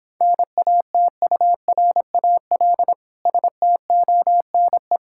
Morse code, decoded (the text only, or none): NATURAL STONE